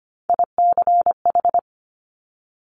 Morse code, heard as a keyed tone, about 25 wpm, 700 Hz. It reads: ICH
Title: Top Three Letter Combinations in Words 25wpm